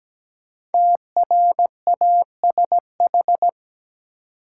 Morse code, heard as a keyed tone, about 17 wpm, 700 Hz. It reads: TRASH
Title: Five Letter Words - 17wpm